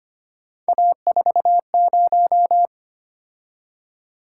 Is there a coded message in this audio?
Yes: Morse code from 0.7 to 2.7 s, heard as a keyed tone, about 32 wpm, 700 Hz. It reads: A40